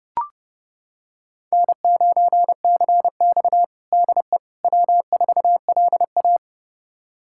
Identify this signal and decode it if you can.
Morse code, heard as a keyed tone, about 32 wpm, 700 Hz. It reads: N9CX DE W4LA
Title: USA Callsign DE Callsign - 4 Characters - 30wpm